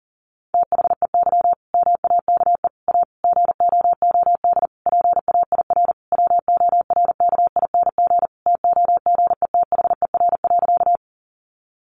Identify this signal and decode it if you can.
Morse code, heard as a keyed tone, about 37 wpm, 700 Hz. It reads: THEY MAKE A GOOD PAIR WORKING TOGETHER.
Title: Sentences from Top 1200 Words 40wpm